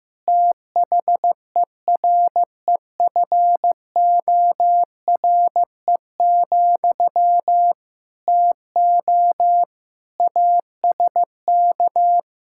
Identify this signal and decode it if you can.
Morse code, heard as a keyed tone, about 14 wpm, 700 Hz. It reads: THEREFORE, TO ASK
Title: Sentences from Top 1000 Words - Review Part 5 - 15wpm